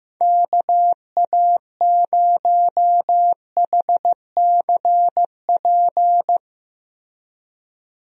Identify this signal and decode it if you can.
Morse code, heard as a keyed tone, about 15 wpm, 700 Hz. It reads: KA0HCP
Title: USA Call Signs - 6 Characters - Repeated 3x - 15wpm